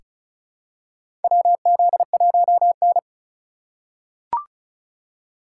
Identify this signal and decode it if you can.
Morse code, heard as a keyed tone, about 36 wpm, 700 Hz. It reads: WZ1N
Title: USA Call Signs - 4 Characters - 4x Character Spacing - 35wpm